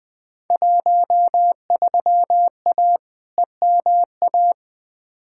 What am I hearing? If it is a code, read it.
Morse code, heard as a keyed tone, about 20 wpm, 700 Hz. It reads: 13A EMA
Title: Field Day Exchanges Encore - 20wpm